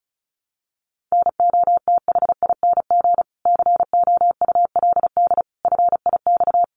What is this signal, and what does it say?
Morse code, heard as a keyed tone, about 34 wpm, 700 Hz. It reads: NOTHING COULD FIX
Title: Sentences from Top 1400 Words 35wpm